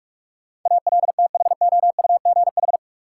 Morse code, heard as a keyed tone, about 46 wpm, 700 Hz. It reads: ALTHOUGH
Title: Sentences from Top 600 Words - New Word Drill - 45wpm